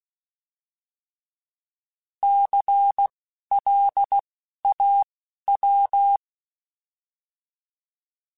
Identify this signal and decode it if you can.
Morse code, heard as a keyed tone, about 16 wpm, spaced as for 11 wpm, 800 Hz. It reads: CLAW
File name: Alphabetical / Alphabetical mp3s / Cs @10